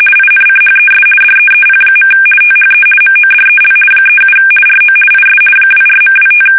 RTTY, about 50 baud. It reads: INCHES DEEP. THE MODEL 37 RO IS 24.25
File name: fsk850-50B / teletype-corp